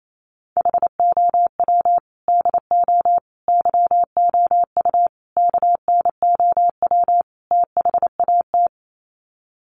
Morse code, heard as a keyed tone, about 28 wpm, 690 Hz. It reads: HOW DO YOU KNOW THAT
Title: Sets of 5 Words Encore 28wpm